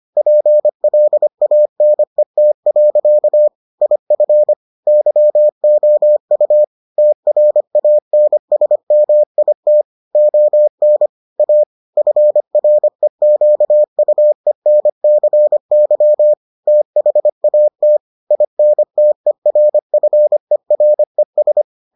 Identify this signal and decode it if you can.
Morse code, heard as a keyed tone, about 25 wpm, 600 Hz. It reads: PLANET. IF YOU TRANSMIT ON A FREQUENCY THAT INTERFERES